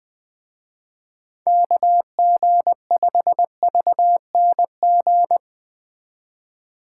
Morse code, heard as a keyed tone, about 20 wpm, 700 Hz. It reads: KG5VNG
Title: Speed Racing - USA 2x3 Call Signs 17wpm